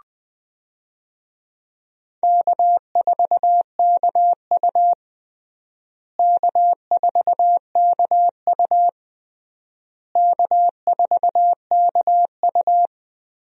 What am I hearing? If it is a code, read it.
Morse code, heard as a keyed tone, about 21 wpm, 700 Hz. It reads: K4KU K4KU K4KU
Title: USA Call Signs - 4 Characters - Repeated 3x - 20wpm